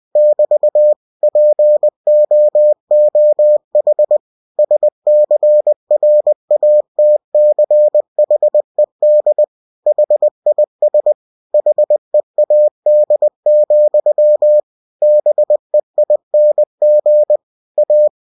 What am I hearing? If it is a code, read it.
Morse code, heard as a keyed tone, about 20 wpm, 600 Hz. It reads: <BT> POOH SCRATCHED HIS HEAD, BEING A
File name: cw-books / winnie-the-pooh / winnie-20wpm-page-0047